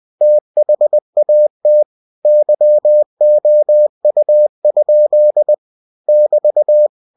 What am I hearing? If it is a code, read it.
Morse code, heard as a keyed tone, about 20 wpm, 600 Hz. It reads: THAT YOU? <BT>
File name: cw-books / winnie-the-pooh / winnie-20wpm-page-0014